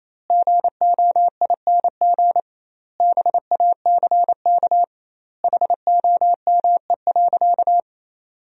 Morse code, heard as a keyed tone, about 28 wpm, 700 Hz. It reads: GOING BACK HOME.
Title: 2x Word-Spacing - Sentences from Top 500 Words - 28wpm